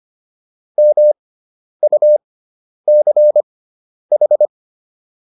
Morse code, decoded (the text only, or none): MUCH